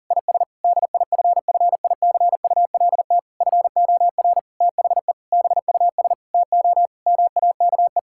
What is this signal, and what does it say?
Morse code, heard as a keyed tone, about 40 wpm, 700 Hz. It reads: IS DIFFICULT FOR THE BUS TO MAKE
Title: Sentences from Top 1000 Words - Review Part 6 - 40wpm